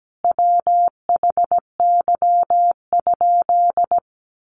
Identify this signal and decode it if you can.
Morse code, heard as a keyed tone, about 17 wpm, 700 Hz. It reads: WHY?
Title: Sentences from Top 200 Words 17wpm